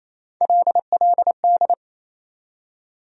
Morse code, heard as a keyed tone, about 28 wpm, 700 Hz. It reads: LLD
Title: USA Call Sign Postfixes - 3 Letters 28wpm